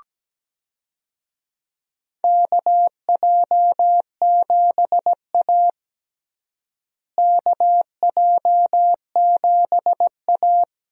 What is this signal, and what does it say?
Morse code, heard as a keyed tone, about 17 wpm, 700 Hz. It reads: KJ7A KJ7A
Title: USA Call Signs - 4 Characters - Repeated 3x - 17wpm